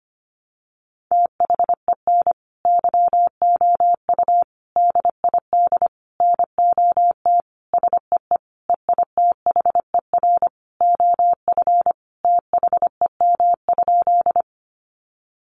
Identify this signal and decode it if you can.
Morse code, heard as a keyed tone, about 25 wpm, 700 Hz. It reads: THEN YOU DID NOT SEE EITHER OF THEM?
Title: Sentences from Top 600 Words 25wpm